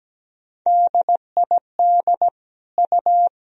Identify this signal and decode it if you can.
Morse code, heard as a keyed tone, about 17 wpm, 700 Hz. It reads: DID U
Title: Sentences from Top 100 Words and Abbreviations 17wpm